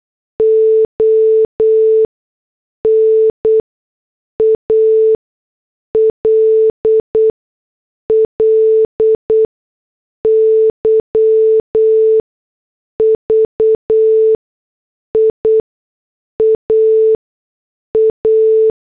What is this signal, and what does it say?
Morse code, heard as a keyed tone, about 8 wpm, spaced as for 6 wpm, 440 Hz. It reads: ONALLY VIA A